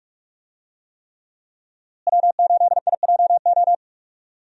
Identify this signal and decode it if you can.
Morse code, heard as a keyed tone, about 46 wpm, 700 Hz. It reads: W8IJO
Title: USA Call Signs - 5 Characters - Repeated 3x - 45wpm